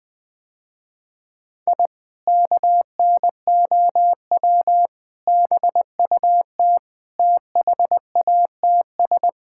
Morse code, heard as a keyed tone, about 20 wpm, 700 Hz. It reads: I KNOW BUT THATS
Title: Sentences with 2-Word Phrases - Spoken Beginning - 20wpm